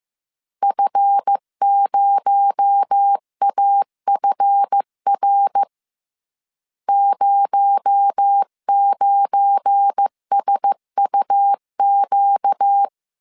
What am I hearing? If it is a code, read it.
Morse code, heard as a keyed tone, about 15 wpm, 800 Hz. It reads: F0AFR 09SUQ
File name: CW / DET / 1 9 1500C S C 14wpm 3245